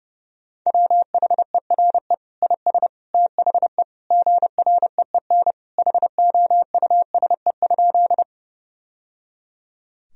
Morse code, heard as a keyed tone, about 30 wpm, 700 Hz. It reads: WHERE IS THE GREEN HOUSE?
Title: Sentences from Top 1000 Words - Review Part 2 - 30wpm